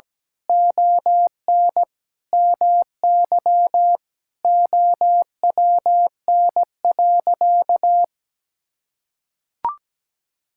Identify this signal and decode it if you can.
Morse code, heard as a keyed tone, about 17 wpm, 700 Hz. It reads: ON MY OWN.
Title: Sentences from Top 1000 Words - Review Part 6 - 17wpm